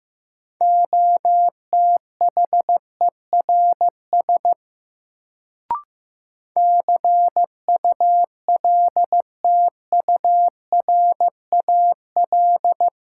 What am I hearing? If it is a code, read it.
Morse code, heard as a keyed tone, about 15 wpm, 700 Hz. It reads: OTHERS CULTURAL
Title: Sentences from Top 1500 Words - New Word Drill Alt - 15wpm